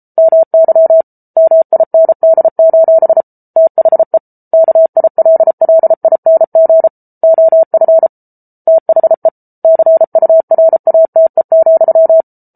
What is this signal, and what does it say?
Morse code, heard as a keyed tone, about 34 wpm, 650 Hz. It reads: MY MIND: THE KILLING OF THE CURATE,